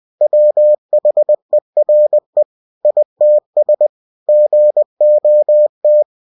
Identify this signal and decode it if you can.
Morse code, heard as a keyed tone, about 20 wpm, 600 Hz. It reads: WHERE ITS GOT
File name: cw-books / winnie-the-pooh / winnie-20wpm-page-0039